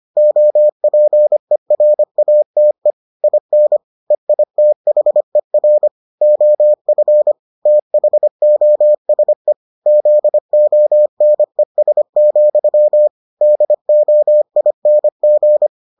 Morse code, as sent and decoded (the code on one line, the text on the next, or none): --- .--. .-. .- - . / .. -. / . .. - .... .-. / --- ..-. / - .... --- ... / --.. --- -. ... --..-- / -.. --- .. -. --.
OPERATE IN EITHER OF THOSE ZONES, DOING